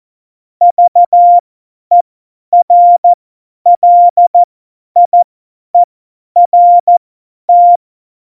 Morse code, heard as a keyed tone, about 14 wpm, spaced as for 10 wpm, 700 Hz. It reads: VERLIERT
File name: Morsekurs / Witze / Witz 24. Oktober 17, Tempo 50